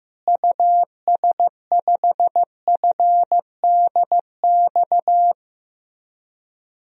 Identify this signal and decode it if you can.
Morse code, heard as a keyed tone, about 15 wpm, 700 Hz. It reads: US5FDX
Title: DXCC-100 Call Signs - 6 Characters - Repeated 2x - 15wpm